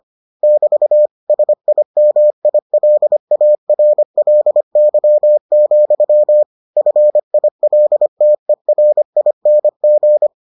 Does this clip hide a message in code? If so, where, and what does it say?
Yes: Morse code from 0.4 to 10.3 s, heard as a keyed tone, about 25 wpm, 600 Hz. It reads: <BT> SIMILARLY, FILTERING